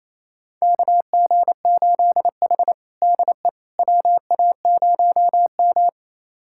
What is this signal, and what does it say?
Morse code, heard as a keyed tone, about 28 wpm, 700 Hz. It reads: KG8H DE WA0M
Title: USA Callsign DE Callsign - 4 Characters - 28wpm